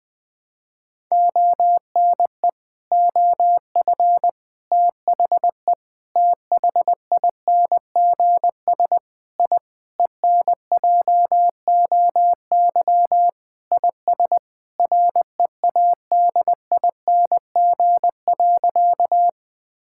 Morse code, heard as a keyed tone, about 20 wpm, 700 Hz. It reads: ONE OF THE THINGS I ENJOY IS READING.
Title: Sentences with 5-Word Phrases - 20wpm